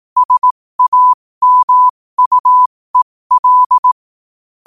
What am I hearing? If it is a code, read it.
Morse code, heard as a keyed tone, about 18 wpm, spaced as for 15 wpm, 1000 Hz. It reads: SAMUEL
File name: qotdfiles / 15wpm